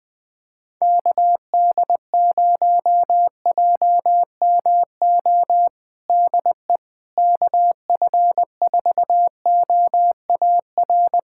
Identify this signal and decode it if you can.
Morse code, heard as a keyed tone, about 20 wpm, 700 Hz. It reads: KD0JMO DE KF4OAR
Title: USA Callsign DE Callsign - 6 Characters - 20wpm